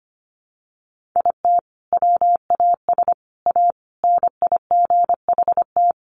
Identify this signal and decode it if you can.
Morse code, heard as a keyed tone, about 25 wpm, 700 Hz. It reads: IT WAS A NIGHT
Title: Sentences from Top 1500 Words 25wpm